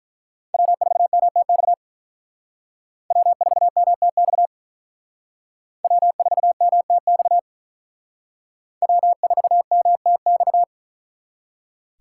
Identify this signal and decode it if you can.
Morse code, heard as a keyed tone, about 44 wpm, 700 Hz. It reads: W4MTX W4MTX W4MTX W4MTX